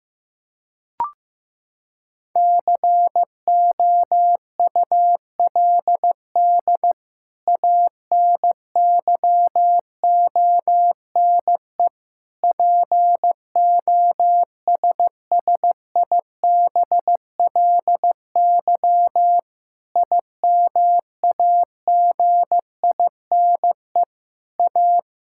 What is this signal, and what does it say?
Morse code, heard as a keyed tone, about 15 wpm, 700 Hz. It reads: COULD ANYONE POSSIBLY IMAGINE A
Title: Sentences from Top 1600 Words 15wpm